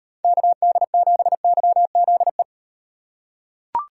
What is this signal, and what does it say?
Morse code, heard as a keyed tone, about 38 wpm, 700 Hz. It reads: KD7YZE